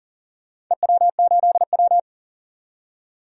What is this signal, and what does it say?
Morse code, heard as a keyed tone, about 40 wpm, 700 Hz. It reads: EW8W
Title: DXCC-100 Call Signs - 4 Characters - 40wpm